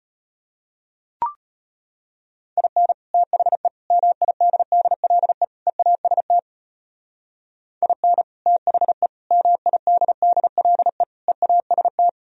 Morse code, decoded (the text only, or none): IN THE MIDDLE EAST IN THE MIDDLE EAST